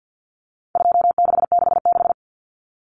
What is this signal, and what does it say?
Morse code, heard as a keyed tone, about 50 wpm, 700 Hz. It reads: 2666
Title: Common Cut Numbers - 4-Digit 50wpm